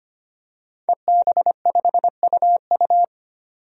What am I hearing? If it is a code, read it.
Morse code, heard as a keyed tone, about 25 wpm, 700 Hz. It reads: EB5UU
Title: DXCC-100 Call Signs - 5 Characters - 25wpm